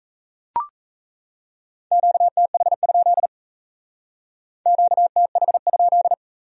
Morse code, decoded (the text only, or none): QTH? QTH?